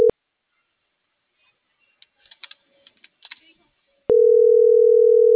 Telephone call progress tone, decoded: ringback tone

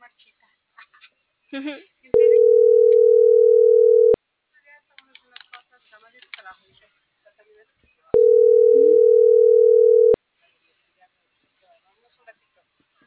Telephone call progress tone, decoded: ringback tone

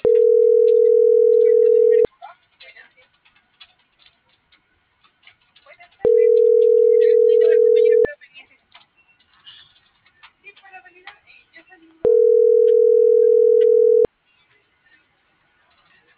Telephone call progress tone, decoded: ringback tone

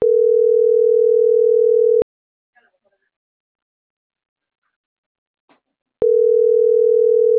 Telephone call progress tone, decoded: ringback tone

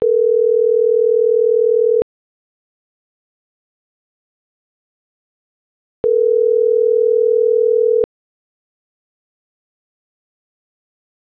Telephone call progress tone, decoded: ringback tone

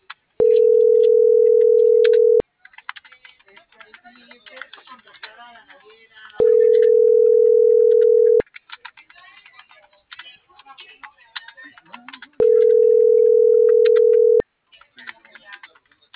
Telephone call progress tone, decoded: ringback tone

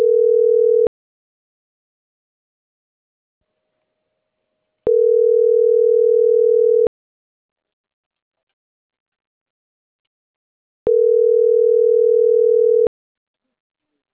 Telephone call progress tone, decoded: ringback tone

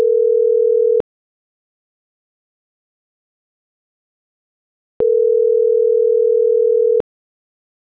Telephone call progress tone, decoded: ringback tone

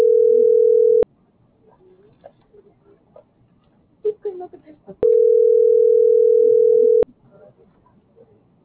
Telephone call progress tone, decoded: ringback tone

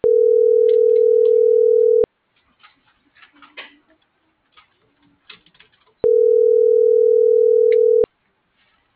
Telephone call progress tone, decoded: ringback tone

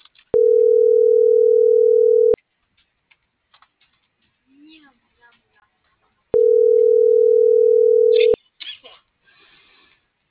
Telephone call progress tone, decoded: ringback tone

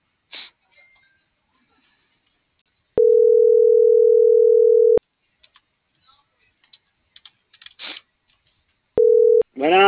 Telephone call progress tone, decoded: ringback tone

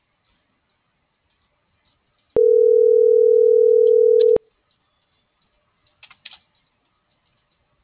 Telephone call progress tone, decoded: ringback tone

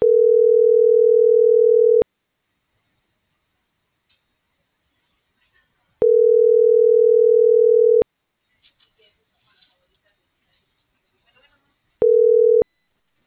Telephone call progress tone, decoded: ringback tone

